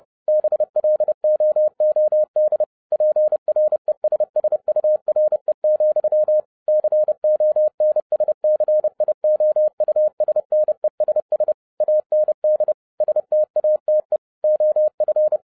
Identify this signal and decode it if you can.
Morse code, heard as a keyed tone, about 30 wpm, 610 Hz. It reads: BLOOD PRESSURE, CONSCIOUSNESS AND STATE OF
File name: 2011-07-mp3 / n55 BACK FROM A BIG ACCIDENT 30wpm